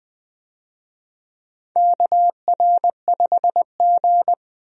Morse code, heard as a keyed tone, about 20 wpm, 700 Hz. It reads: KR5G
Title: USA Call Signs - 4 Characters - Repeated 2x - 20wpm